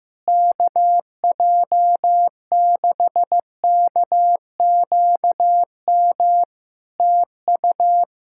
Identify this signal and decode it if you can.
Morse code, heard as a keyed tone, about 15 wpm, 700 Hz. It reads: KJ6KQM TU
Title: Practice QSO - Exchange 4 - Ending - 15wpm